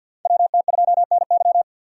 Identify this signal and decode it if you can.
Morse code, heard as a keyed tone, about 48 wpm, 700 Hz. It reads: WT2NY
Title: USA Call Signs - 4x Character Spacing - 50wpm